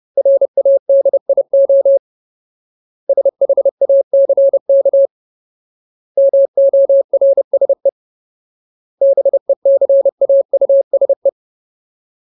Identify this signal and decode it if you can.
Morse code, heard as a keyed tone, about 30 wpm, 550 Hz. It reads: RADIO SHACK MORSE BECAUSE
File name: Ultimate Word List / 30WPM Ultimate